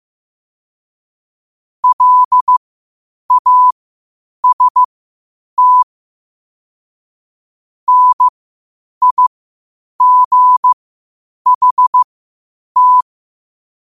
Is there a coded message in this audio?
Yes: Morse code from 1.8 to 13.0 s, heard as a keyed tone, about 15 wpm, spaced as for 8 wpm, 1000 Hz. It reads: LAST NIGHT